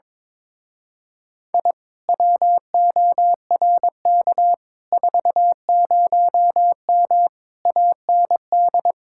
Morse code, heard as a keyed tone, about 22 wpm, 700 Hz. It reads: I WORK 40M AND
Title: Speed-Racing Common QSO Phrases - 22wpm